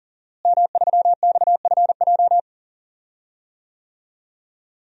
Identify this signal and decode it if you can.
Morse code, heard as a keyed tone, about 40 wpm, 700 Hz. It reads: M3XFJ
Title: DXCC-100 Call Signs - 5 Characters - Repeated 2x - 40wpm